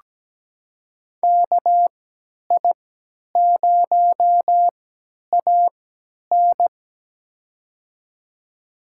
Morse code, heard as a keyed tone, about 17 wpm, spaced as for 10 wpm, 700 Hz. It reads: KI0AN
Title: USA Call Signs - 5 Characters - 3x Character Spacing - 17wpm